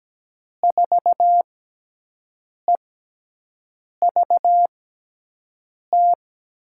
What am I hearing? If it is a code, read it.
Morse code, heard as a keyed tone, about 17 wpm, spaced as for 6 wpm, 700 Hz. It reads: 4EVT
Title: USA Call Signs - 6x Character Spacing - 17wpm